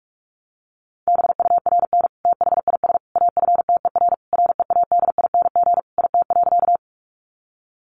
Morse code, heard as a keyed tone, about 44 wpm, 700 Hz. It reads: BURN THIS AFTER READING IT.
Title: Sentences from Top 1300 Words 45wpm